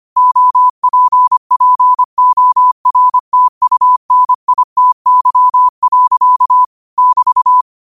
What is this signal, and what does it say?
Morse code, heard as a keyed tone, about 25 wpm, 1000 Hz. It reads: OPPORTUNITY. <BT>
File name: qotdfiles / 25wpm